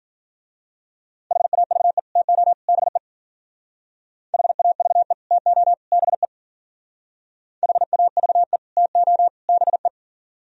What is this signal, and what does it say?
Morse code, heard as a keyed tone, about 45 wpm, 700 Hz. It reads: HAVE TO BE HAVE TO BE HAVE TO BE
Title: Speed-Racing - Sets of 3 Words 40wpm